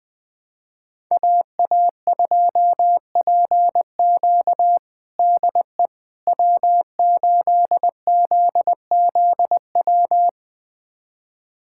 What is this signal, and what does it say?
Morse code, heard as a keyed tone, about 20 wpm, 700 Hz. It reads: AA2PQ DE W8ZZW
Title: USA Callsign DE Callsign 20wpm